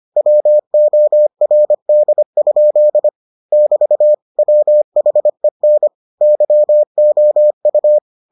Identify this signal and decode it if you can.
Morse code, heard as a keyed tone, about 25 wpm, 600 Hz. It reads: WORD? <BT> WHEN YOU